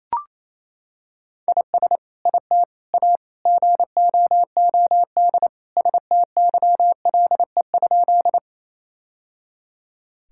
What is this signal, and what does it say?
Morse code, heard as a keyed tone, about 28 wpm, 700 Hz. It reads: IS IT A GOOD STYLE?
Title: Sentences from Top 1000 Words - Review Part 5 - 28wpm